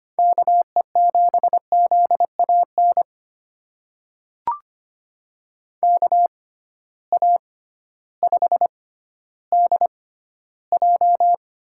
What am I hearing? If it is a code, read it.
Morse code, heard as a keyed tone, about 25 wpm, spaced as for 13 wpm, 700 Hz. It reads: KE7ZAN KA5DJ
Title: USA Call Signs - 6x Character Spacing - 25wpm